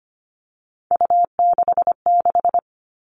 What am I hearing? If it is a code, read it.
Morse code, heard as a keyed tone, about 25 wpm, 700 Hz. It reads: U66
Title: Three Challenging Letter-Number Combinations 25wpm